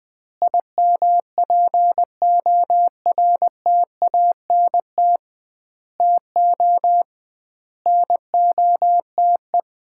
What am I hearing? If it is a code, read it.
Morse code, heard as a keyed tone, about 20 wpm, 700 Hz. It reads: IMPORTANT TO NOTE
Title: 2x Word-Spacing - Sets of 5 Words - 20wpm